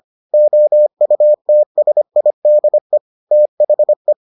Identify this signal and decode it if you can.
Morse code, heard as a keyed tone, about 25 wpm, 600 Hz. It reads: OUTSIDE THE